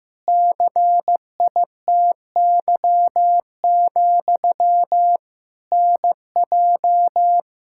Morse code, heard as a keyed tone, about 15 wpm, 700 Hz. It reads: CITY, NJ
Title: Practice QSO - Exchange 1 - Beginning - 15wpm